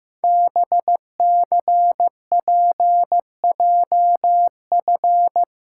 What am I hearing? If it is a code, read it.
Morse code, heard as a keyed tone, about 15 wpm, 700 Hz. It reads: BCPJF
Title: Code Groups - 5 Letters - 15wpm